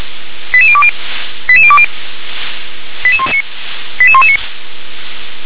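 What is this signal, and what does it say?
Five-tone selective call (ZVEI), heard as quick runs of five tones, tones of about 70 ms. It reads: 88020 88020 88119
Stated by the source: ZVEI five-tone selective call 88645 received by SDR.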